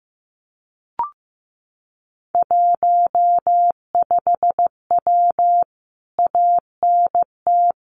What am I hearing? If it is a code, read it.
Morse code, heard as a keyed tone, about 15 wpm, 700 Hz. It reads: 15W ANT
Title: Common QSO Phrases Encore 15wpm